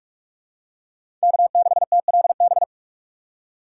Morse code, heard as a keyed tone, about 45 wpm, 700 Hz. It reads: K6TLB
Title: USA Call Signs - 5 Characters - 3x Character Spacing - 45wpm